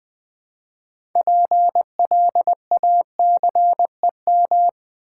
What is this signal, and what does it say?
Morse code, heard as a keyed tone, about 20 wpm, 700 Hz. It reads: PLACEM
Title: Build A Word - 3 Letters At A Time - 9-Letter Words - 20wpm